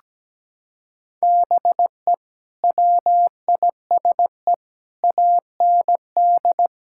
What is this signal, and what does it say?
Morse code, heard as a keyed tone, about 17 wpm, 700 Hz. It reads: BE WISE AND
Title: Sentences from Top 1600 Words 17wpm